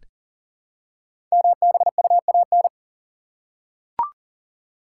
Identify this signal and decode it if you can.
Morse code, heard as a keyed tone, about 40 wpm, 700 Hz. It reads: MBUAN
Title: Code Groups - 5 Letters - 40wpm